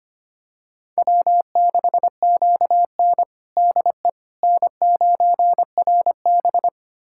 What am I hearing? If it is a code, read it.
Morse code, heard as a keyed tone, about 25 wpm, 700 Hz. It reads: W6QN DE N9RB
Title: USA Callsign DE Callsign - 4 Characters - 25wpm